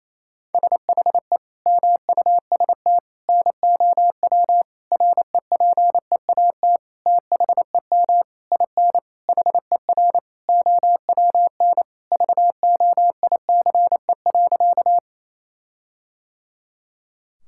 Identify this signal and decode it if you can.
Morse code, heard as a keyed tone, about 28 wpm, 700 Hz. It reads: SHE MUST NOW REPEAT THEM IN HER OWN VOICE.
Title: Sentences from Top 1600 Words 28wpm